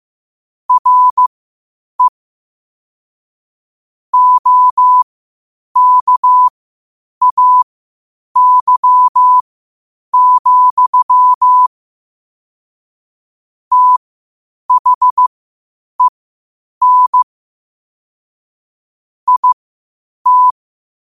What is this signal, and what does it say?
Morse code, heard as a keyed tone, about 15 wpm, spaced as for 8 wpm, 1000 Hz. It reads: RE OKAY, THEN IT
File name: qotdfiles / 8wpm